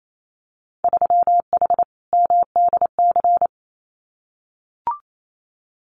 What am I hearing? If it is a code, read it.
Morse code, heard as a keyed tone, about 28 wpm, 700 Hz. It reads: 3H MDC
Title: Winter Field Day Exchanges Encore 28wpm